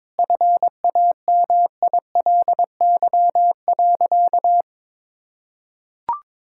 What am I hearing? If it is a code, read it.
Morse code, heard as a keyed tone, about 22 wpm, 700 Hz. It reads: FAMILY.
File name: practice / 22wpm / Sentences with 3-Word Phrases - 22wpm